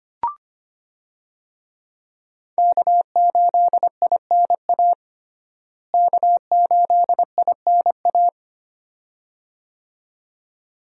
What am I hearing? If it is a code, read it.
Morse code, heard as a keyed tone, about 25 wpm, 700 Hz. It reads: K8INA K8INA
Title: USA Call Signs - 5 Characters - Repeated 2x - 25wpm